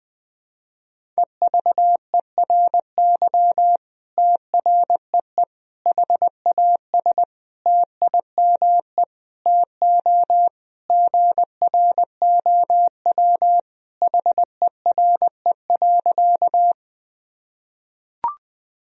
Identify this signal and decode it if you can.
Morse code, heard as a keyed tone, about 20 wpm, 700 Hz. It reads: EVERY TREE HAS TIME TO GROW HERE.
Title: Sentences from Top 1000 Words - Review Part 1 - 20wpm